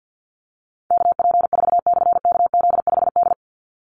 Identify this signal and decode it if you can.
Morse code, heard as a keyed tone, about 50 wpm, 700 Hz. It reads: KP4/KZ5D
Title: Call Signs - 50wpm